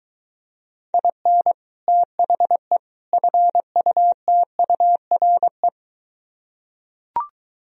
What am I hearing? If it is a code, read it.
Morse code, heard as a keyed tone, about 23 wpm, 700 Hz. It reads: IN THE FUTURE